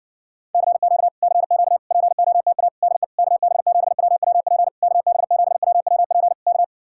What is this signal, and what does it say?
Morse code, heard as a keyed tone, about 69 wpm, 700 Hz. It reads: CQ CQ POTA DE KB7WWP KB7WWP K